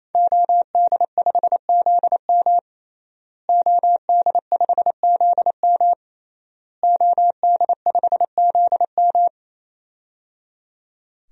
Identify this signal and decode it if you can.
Morse code, heard as a keyed tone, about 28 wpm, 700 Hz. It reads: OD5ZM OD5ZM OD5ZM
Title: DXCC-100 Call Signs - 5 Characters - Repeated 3x - 28wpm